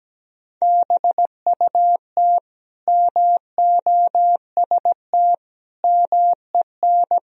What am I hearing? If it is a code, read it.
Morse code, heard as a keyed tone, about 17 wpm, 700 Hz. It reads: BUT MOST MEN
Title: Sentences from Top 1000 Words - Review Part 1 - 17wpm